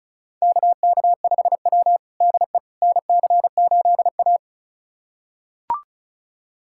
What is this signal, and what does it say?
Morse code, heard as a keyed tone, about 36 wpm, 700 Hz. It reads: KK5W DE NC8A
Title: USA Callsign DE Callsign - 4 Characters - 35wpm